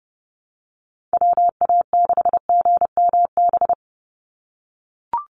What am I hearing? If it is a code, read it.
Morse code, heard as a keyed tone, about 30 wpm, 700 Hz. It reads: WA6GMB
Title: USA Call Signs 30wpm